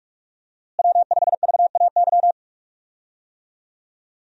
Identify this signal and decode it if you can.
Morse code, heard as a keyed tone, about 42 wpm, 700 Hz. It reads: W5VAY
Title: USA Call Signs - 5 Characters - Repeated 2x - 45wpm